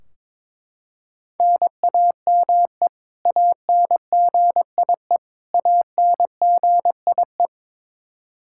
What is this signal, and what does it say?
Morse code, heard as a keyed tone, about 21 wpm, 700 Hz. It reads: NAME ANGIE ANGIE
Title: Practice QSO - Exchange 1 - Beginning - 22wpm